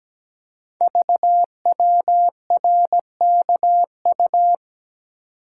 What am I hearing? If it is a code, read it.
Morse code, heard as a keyed tone, about 17 wpm, 700 Hz. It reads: VWRKU
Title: Code Groups - 5 Letter:Numbers - 17wpm